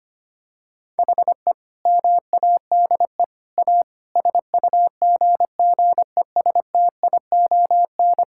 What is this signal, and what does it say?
Morse code, heard as a keyed tone, about 25 wpm, 700 Hz. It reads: HE MADE A SUGGESTION
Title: Sentences from Top 1600 Words 25wpm